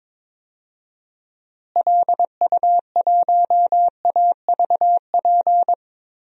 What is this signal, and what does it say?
Morse code, heard as a keyed tone, about 22 wpm, 700 Hz. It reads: LU1AVP
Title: DXCC-100 Call Signs - 6 Characters - Repeated 3x - 22wpm